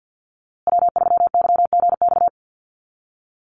Morse code, heard as a keyed tone, about 50 wpm, 700 Hz. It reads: W3YGX
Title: USA Call Signs Repeated 3x 50wpm